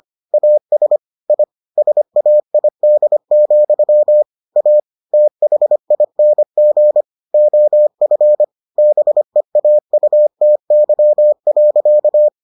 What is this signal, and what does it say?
Morse code, heard as a keyed tone, about 25 wpm, 600 Hz. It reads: AS I SAID, A THING OF BEAUTY.